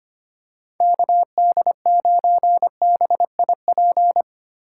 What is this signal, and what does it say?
Morse code, heard as a keyed tone, about 25 wpm, 700 Hz. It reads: KD9BIP
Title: USA Call Signs - 6 Characters - 3x Character Spacing - 25wpm